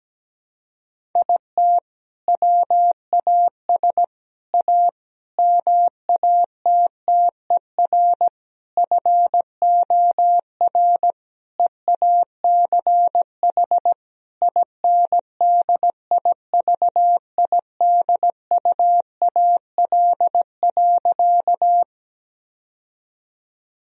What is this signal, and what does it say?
Morse code, heard as a keyed tone, about 17 wpm, 700 Hz. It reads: IT WAS A MATTER FOR EACH INDIVIDUAL.
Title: Sentences from Top 1000 Words - Review Part 4 - 17wpm